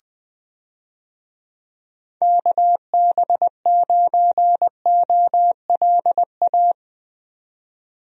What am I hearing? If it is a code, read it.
Morse code, heard as a keyed tone, about 21 wpm, 700 Hz. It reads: KB9OLA
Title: USA Call Signs - 6 Characters - 20wpm